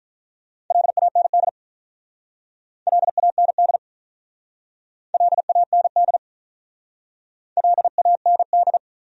Morse code, heard as a keyed tone, about 45 wpm, 700 Hz. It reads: LAND LAND LAND LAND